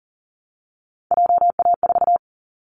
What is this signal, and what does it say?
Morse code, heard as a keyed tone, about 40 wpm, 700 Hz. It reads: JA4
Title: Callsign Prefixes for Top 25 DXCC - 40wpm